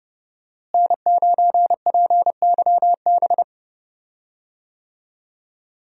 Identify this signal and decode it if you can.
Morse code, heard as a keyed tone, about 30 wpm, 700 Hz. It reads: N9PYB
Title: USA Call Signs - 5 Characters - Repeated 3x - 30wpm